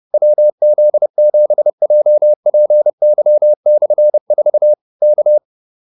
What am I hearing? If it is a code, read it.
Morse code, heard as a keyed tone, about 30 wpm, 600 Hz. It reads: WZ7JPY/4 K